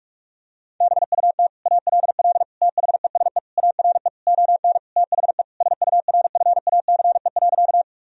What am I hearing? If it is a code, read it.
Morse code, heard as a keyed tone, about 46 wpm, 700 Hz. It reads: BUT ALL THESE ARE ON THE SURFACE.
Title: Sentences from Top 1000 Words - Review Part 3 - 45wpm